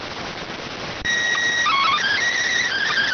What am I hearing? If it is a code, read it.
SSTV (slow-scan television), PD 290.